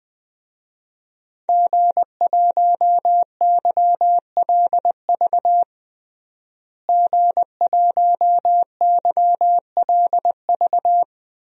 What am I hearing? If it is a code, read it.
Morse code, heard as a keyed tone, about 21 wpm, 700 Hz. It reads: G1YLV G1YLV
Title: DXCC-100 Call Signs - 5 Characters - Repeated 3x - 20wpm